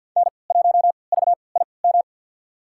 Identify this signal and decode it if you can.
Morse code, heard as a keyed tone, about 47 wpm, spaced as for 28 wpm, 700 Hz. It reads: N1VIM
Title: USA Call Signs - 5 Characters - 3x Character Spacing - 50wpm